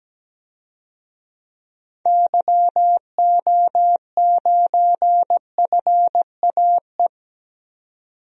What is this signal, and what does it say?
Morse code, heard as a keyed tone, about 17 wpm, 700 Hz. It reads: YO9FAE
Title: DXCC-100 Call Signs - 6 Characters - 17wpm